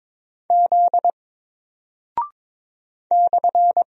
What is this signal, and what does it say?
Morse code, heard as a keyed tone, about 22 wpm, 700 Hz. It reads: Z /
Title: Single Character (Letter:Number:Punctuation:Prosign) Alt 22wpm